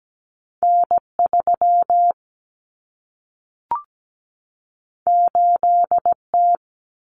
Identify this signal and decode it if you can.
Morse code, heard as a keyed tone, about 17 wpm, 700 Hz. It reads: N3 8T
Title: Common Cut Numbers - 2-Digit 17wpm